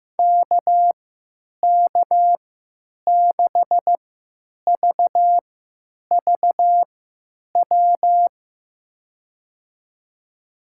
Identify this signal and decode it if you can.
Morse code, heard as a keyed tone, about 15 wpm, spaced as for 9 wpm, 700 Hz. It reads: KK6VVW
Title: USA Call Signs - 6 Characters - 3x Character Spacing - 15wpm